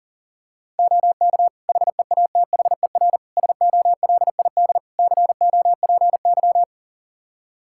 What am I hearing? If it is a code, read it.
Morse code, heard as a keyed tone, about 37 wpm, 700 Hz. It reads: OK HEATHER SOLID COPY
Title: K1USN SST QSO Practice 40wpm